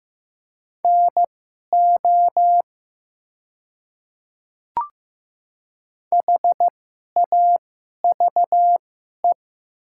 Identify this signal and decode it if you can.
Morse code, heard as a keyed tone, about 15 wpm, spaced as for 11 wpm, 700 Hz. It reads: NO HAVE
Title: 2x Character Spacing - Top 100 Words - 15wpm